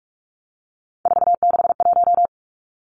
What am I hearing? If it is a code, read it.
Morse code, heard as a keyed tone, about 46 wpm, 700 Hz. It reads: 461
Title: Three Challenging Letter-Number Combinations 45wpm